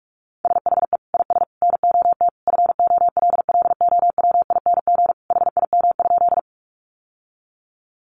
Morse code, heard as a keyed tone, about 44 wpm, 700 Hz. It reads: SHE IS NOT FOLLOWING HIM?
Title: Sentences from Top 700 Words 45wpm